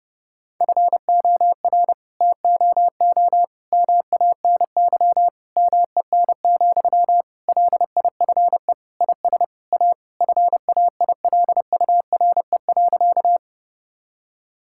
Morse code, heard as a keyed tone, about 30 wpm, 700 Hz. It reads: FOR TOO MANY MEN, LIFE IS A FAILURE.
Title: Sentences from Top 1000 Words - Review Part 6 - 30wpm